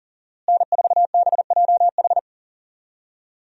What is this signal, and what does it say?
Morse code, heard as a keyed tone, about 40 wpm, 700 Hz. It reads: N4BJH
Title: USA Call Signs - 5 Characters - Repeated 3x - 40wpm